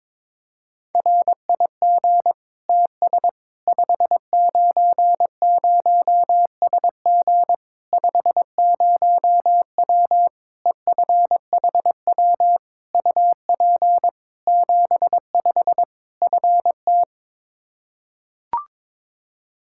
Morse code, heard as a keyed tone, about 22 wpm, 700 Hz. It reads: RIG TS 590SG 50W EFHW UP 75 FT
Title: Practice QSO - Exchange 2 - Info - 22wpm